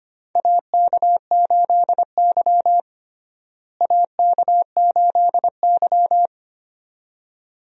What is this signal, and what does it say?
Morse code, heard as a keyed tone, about 25 wpm, 700 Hz. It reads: AK8Y AK8Y
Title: USA Call Signs - 4 Characters - Repeated 2x - 25wpm